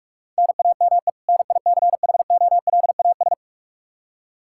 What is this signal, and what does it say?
Morse code, heard as a keyed tone, about 45 wpm, 700 Hz. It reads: NAME NICHOLAS
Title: K1USN SST QSO Practice 45wpm